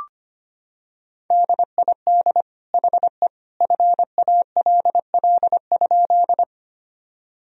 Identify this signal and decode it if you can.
Morse code, heard as a keyed tone, about 25 wpm, 700 Hz. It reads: DID HE FALL?
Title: Sentences from Top 1000 Words - Review Part 2 - 25wpm